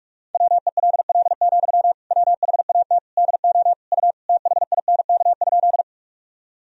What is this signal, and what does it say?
Morse code, heard as a keyed tone, about 46 wpm, 700 Hz. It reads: WELL, WHAT DO U THINK?
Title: Sentences from Top 200 Words and Abbreviations 45wpm